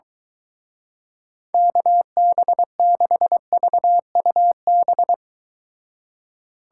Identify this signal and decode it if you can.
Morse code, heard as a keyed tone, about 23 wpm, 700 Hz. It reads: KB6VUB